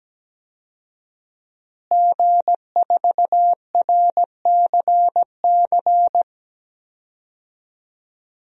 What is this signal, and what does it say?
Morse code, heard as a keyed tone, about 17 wpm, 700 Hz. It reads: G4RCC
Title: DXCC-100 Call Signs - 5 Characters - Repeated 2x - 17wpm